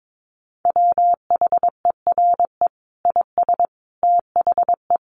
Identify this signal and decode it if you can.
Morse code, heard as a keyed tone, about 22 wpm, 700 Hz. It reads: WHERE IS THE
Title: Sentences from Top 1300 Words 22wpm